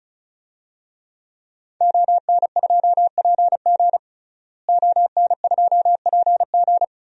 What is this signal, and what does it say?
Morse code, heard as a keyed tone, about 34 wpm, 700 Hz. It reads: ON2PG ON2PG
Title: DXCC-100 Call Signs - 5 Characters - Repeated 3x - 35wpm